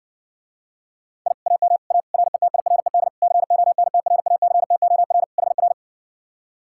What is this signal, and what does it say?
Morse code, heard as a keyed tone, about 59 wpm, 700 Hz. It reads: I AM A RETIRED CONTRACTOR HR